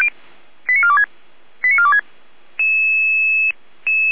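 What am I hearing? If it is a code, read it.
Five-tone selective call (ZVEI), heard as quick runs of five tones, tones of about 70 ms. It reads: 89426 89426
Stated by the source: ZVEI five-tone call received by SDR, recorded by Feuerwehr Orlamünde.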